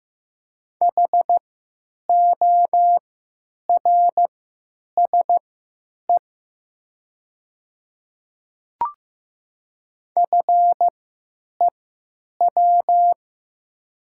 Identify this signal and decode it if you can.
Morse code, heard as a keyed tone, about 15 wpm, spaced as for 8 wpm, 700 Hz. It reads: HORSE FEW
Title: 3x Character Spacing - Top 500 Words - 15wpm